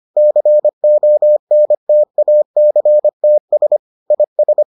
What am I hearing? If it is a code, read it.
Morse code, heard as a keyed tone, about 25 wpm, 600 Hz. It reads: CONTACTS IS